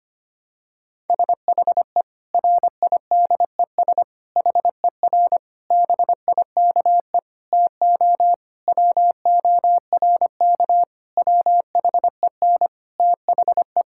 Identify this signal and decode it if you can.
Morse code, heard as a keyed tone, about 25 wpm, 700 Hz. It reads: SHE RIDES HER BIKE TO WORK WHEN THE
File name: practice / 25wpm / Difficult Sentences with Hints - 25wpm